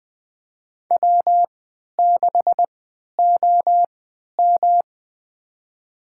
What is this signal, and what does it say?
Morse code, heard as a keyed tone, about 20 wpm, spaced as for 11 wpm, 700 Hz. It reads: W6OM
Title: USA Call Signs - 4 Characters - 3x Character Spacing - 20wpm